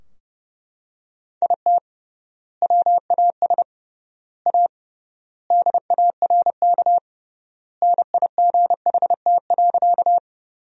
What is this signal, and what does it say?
Morse code, heard as a keyed tone, about 30 wpm, 700 Hz. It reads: IT WAS A DARK NIGHT.
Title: 3x Word-Spacing - Sentences from Top 500 Words - 30wpm